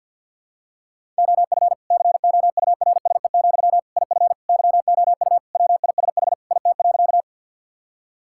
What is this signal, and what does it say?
Morse code, heard as a keyed tone, about 51 wpm, 700 Hz. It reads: OF COURSE, IF YOU WISH IT.
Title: Sentences from Top 1600 Words 50wpm